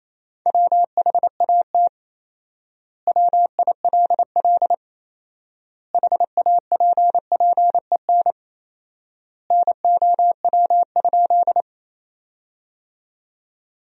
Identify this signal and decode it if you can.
Morse code, heard as a keyed tone, about 27 wpm, 700 Hz. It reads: WHAT WILL HAPPEN NOW?
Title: 4x Word-Spacing - Sentences from Top 400 Words - 28wpm